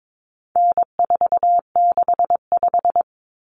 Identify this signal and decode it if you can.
Morse code, heard as a keyed tone, about 22 wpm, 700 Hz. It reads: N465
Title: Common Cut Numbers - 4-Digit 22wpm